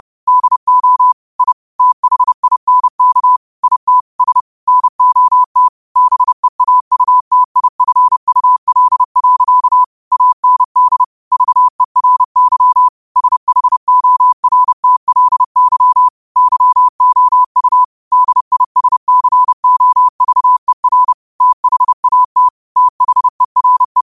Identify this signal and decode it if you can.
Morse code, heard as a keyed tone, about 30 wpm, 1000 Hz. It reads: DO I THINK IT S NOT BEAUTIFUL. AND VERY SHORTLY YOU DISCOVER THAT THERE
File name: qotdfiles / 30wpm